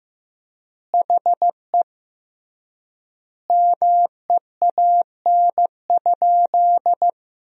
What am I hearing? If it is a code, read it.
Morse code, heard as a keyed tone, about 15 wpm, 700 Hz. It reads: HE MEAN?
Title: 3x Word-Spacing - Sentences from Top 200 Words - 15wpm